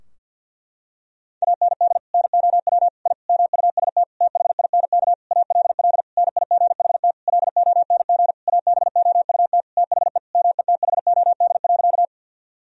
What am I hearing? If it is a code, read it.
Morse code, heard as a keyed tone, about 50 wpm, 700 Hz. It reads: AND NOW I MUST THINK ALL NIGHT LONG ABOUT THE METHOD.
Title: Sentences from Top 1000 Words - Review Part 3 - 50wpm